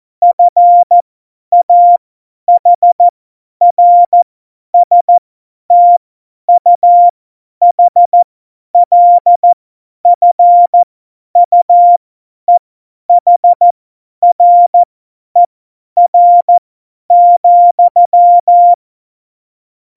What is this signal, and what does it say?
Morse code, heard as a keyed tone, about 14 wpm, spaced as for 10 wpm, 700 Hz. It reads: FAHRSTUHLFUEHRER,
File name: Morsekurs / Witze / Witz 17.Juli 2018, 50 F70